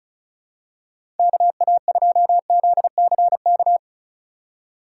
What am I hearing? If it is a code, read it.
Morse code, heard as a keyed tone, about 36 wpm, 700 Hz. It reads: KA2ZCK
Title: USA Call Signs - 4x Character Spacing - 35wpm